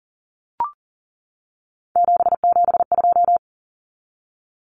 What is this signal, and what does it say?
Morse code, heard as a keyed tone, about 40 wpm, 700 Hz. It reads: 772
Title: Numbers - 3-Digit 40wpm